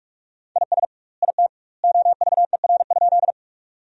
Morse code, heard as a keyed tone, about 46 wpm, 700 Hz. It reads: IS IT OVER?
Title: 2x Word-Spacing - Sentences from Top 100 Words - 45wpm